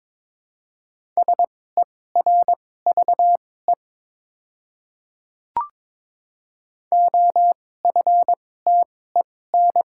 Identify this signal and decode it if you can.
Morse code, heard as a keyed tone, about 22 wpm, spaced as for 16 wpm, 700 Hz. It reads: SERVE OFTEN
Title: 2x Character Spacing - Top 500 Words - 22wpm